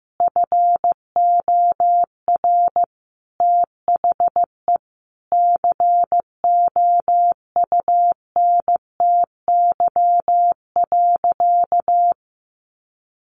Morse code, heard as a keyed tone, about 14 wpm, 700 Hz. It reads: FOR THE COUNTY.
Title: Sentences from Top 1500 Words 15wpm